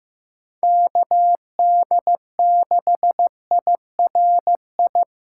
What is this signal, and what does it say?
Morse code, heard as a keyed tone, about 15 wpm, 700 Hz. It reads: KD6IRI
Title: K1USN SST QSO Practice - 15wpm